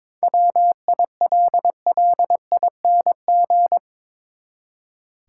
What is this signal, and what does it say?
Morse code, heard as a keyed tone, about 22 wpm, 700 Hz. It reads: WILLING
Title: Seven Letter Words - 22wpm